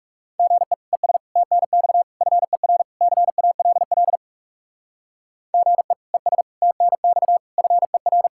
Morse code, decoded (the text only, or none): GE ES TNX FER CALL GE ES TNX FER